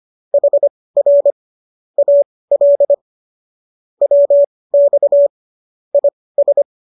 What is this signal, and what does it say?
Morse code, heard as a keyed tone, about 25 wpm, spaced as for 18 wpm, 570 Hz. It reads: HR AL WX IS